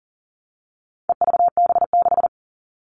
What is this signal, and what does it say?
Morse code, heard as a keyed tone, about 40 wpm, 700 Hz. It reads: EVB6
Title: All Cut Numbers - 4-Digit 40wpm